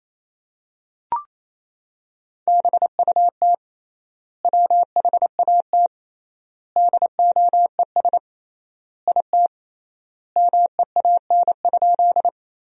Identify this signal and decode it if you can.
Morse code, heard as a keyed tone, about 28 wpm, 700 Hz. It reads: BUT WHAT DOES IT MEAN?
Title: 3x Word-Spacing - Sentences from Top 200 Words - 28wpm